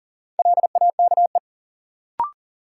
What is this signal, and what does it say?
Morse code, heard as a keyed tone, about 40 wpm, 700 Hz. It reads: LAKE
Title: Sentences from Top 1000 Words - Review Part 3 - New Word Drill - 40wpm